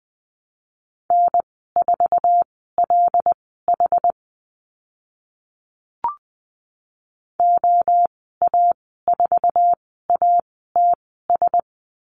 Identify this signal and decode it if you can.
Morse code, heard as a keyed tone, about 20 wpm, spaced as for 15 wpm, 700 Hz. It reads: N4LH OA4ATS
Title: 082 Lesson 14g - Number 2 - Call Signs 20wpm